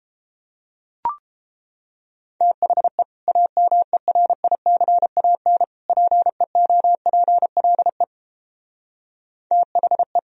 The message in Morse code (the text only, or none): THE AMERICAN PEOPLE THE